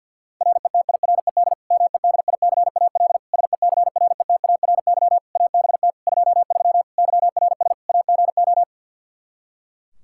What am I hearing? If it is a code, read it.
Morse code, heard as a keyed tone, about 48 wpm, 700 Hz. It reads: RETIRED MEDICAL SECRETARY ABT 23 YRS AGO
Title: K1USN SST QSO Practice 50wpm